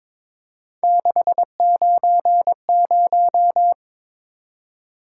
Morse code, heard as a keyed tone, about 22 wpm, 700 Hz. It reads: 690
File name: practice / 22wpm / Three Letter-Number Combinations - 22wpm